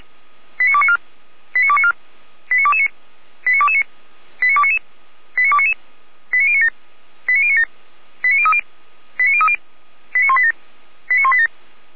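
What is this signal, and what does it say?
Five-tone selective call (ZVEI), heard as quick runs of five tones, tones of about 70 ms. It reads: 89294 89294 89229 89229 89299 89299 89097 89097 89030 89030 89187 89187
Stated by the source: ZVEI five-tone selective call received by SDR, recorded by Feuerwehr Orlamünde.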